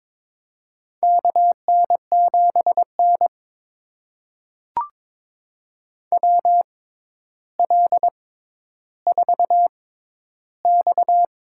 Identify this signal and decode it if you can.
Morse code, heard as a keyed tone, about 22 wpm, spaced as for 11 wpm, 700 Hz. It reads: KN7N WL4X
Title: USA Call Signs - 4 Characters - 6x Character Spacing - 22wpm